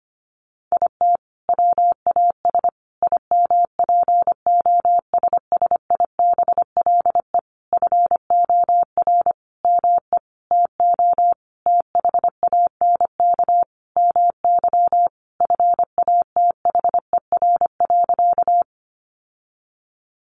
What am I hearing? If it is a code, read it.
Morse code, heard as a keyed tone, about 25 wpm, 700 Hz. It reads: IT WAS IMPOSSIBLE FOR ME TO THANK MY FATHER.
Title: Sentences from Top 1100 Words 25wpm